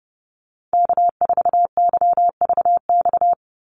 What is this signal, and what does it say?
Morse code, heard as a keyed tone, about 30 wpm, 700 Hz. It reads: K4YVX
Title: USA Call Signs 30wpm